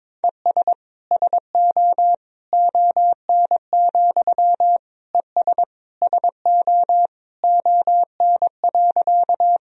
Morse code, heard as a keyed tone, about 22 wpm, 700 Hz. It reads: ES SO ON, ES SO ON.
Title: Sentences from Top 100 Words and Abbreviations 22wpm